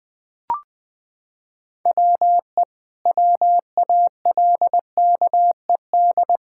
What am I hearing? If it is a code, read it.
Morse code, heard as a keyed tone, about 20 wpm, 700 Hz. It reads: WE WALKED
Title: Sentences with 5-Word Phrases - 20wpm